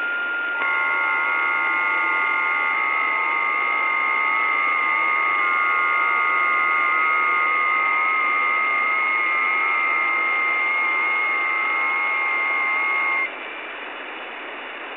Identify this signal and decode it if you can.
FT8.
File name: test / websdr test16